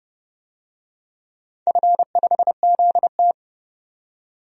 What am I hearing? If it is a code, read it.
Morse code, heard as a keyed tone, about 32 wpm, 700 Hz. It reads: F5ZT
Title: DXCC-100 Call Signs - 4 Characters - 30wpm